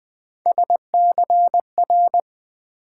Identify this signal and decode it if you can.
Morse code, heard as a keyed tone, about 20 wpm, 700 Hz. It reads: SCR
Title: Build A Word - 3 Letters At A Time - 9-Letter Words - 20wpm